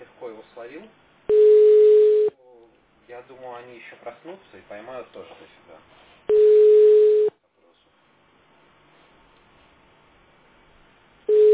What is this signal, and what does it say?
Ringback tone (Europe, 425 Hz), with speech.